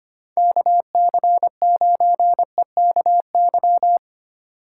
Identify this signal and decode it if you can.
Morse code, heard as a keyed tone, about 25 wpm, 700 Hz. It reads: KC9EKY
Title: USA Call Signs - 6 Characters - 3x Character Spacing - 25wpm